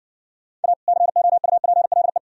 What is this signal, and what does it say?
Morse code, heard as a keyed tone, about 69 wpm, 700 Hz. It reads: A COUPLE